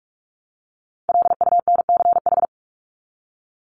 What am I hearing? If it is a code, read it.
Morse code, heard as a keyed tone, about 44 wpm, 700 Hz. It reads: LUNCH
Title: Top 1751-2000 Words 45wpm